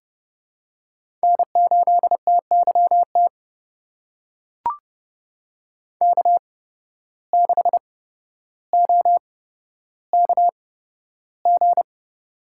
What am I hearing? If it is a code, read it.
Morse code, heard as a keyed tone, about 32 wpm, spaced as for 13 wpm, 700 Hz. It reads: N8TYT K6OKG
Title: USA Call Signs - 8x Character Spacing - 30wpm